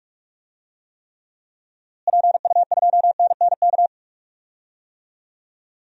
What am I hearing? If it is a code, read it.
Morse code, heard as a keyed tone, about 42 wpm, 700 Hz. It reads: PU2NNK
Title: DXCC-100 Call Signs - 6 Characters - Repeated 2x - 45wpm